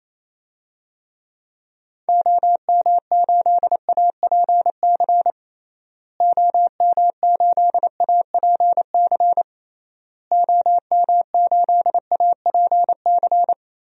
Morse code, heard as a keyed tone, about 27 wpm, 700 Hz. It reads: OM8APC OM8APC OM8APC
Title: DXCC-100 Call Signs - 6 Characters - Repeated 3x - 28wpm